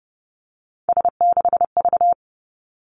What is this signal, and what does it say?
Morse code, heard as a keyed tone, about 30 wpm, 700 Hz. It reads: S6V
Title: Three Challenging Letter-Number Combinations 30wpm